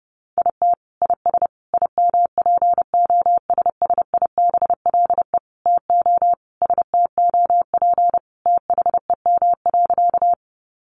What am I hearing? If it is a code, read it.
Morse code, heard as a keyed tone, about 30 wpm, 700 Hz. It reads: IT IS IMPOSSIBLE TO STOP THEM.
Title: Sentences from Top 1100 Words 30wpm